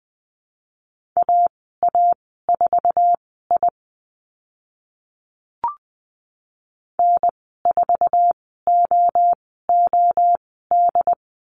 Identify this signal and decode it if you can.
Morse code, heard as a keyed tone, about 20 wpm, spaced as for 14 wpm, 700 Hz. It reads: AA4I N4OOD
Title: 076 Lesson 13g - Letter L - Call Signs 20wpm